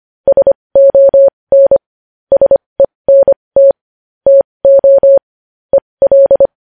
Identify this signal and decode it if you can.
Morse code, heard as a keyed tone, about 25 wpm, spaced as for 20 wpm, 550 Hz. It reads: SON SENT TO EL
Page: news-headlines-in-morse-code-at-20-wpm